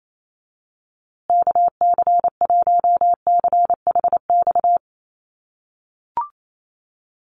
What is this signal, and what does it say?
Morse code, heard as a keyed tone, about 28 wpm, 700 Hz. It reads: KC1CHX
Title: USA Call Signs Repeated 3x 28wpm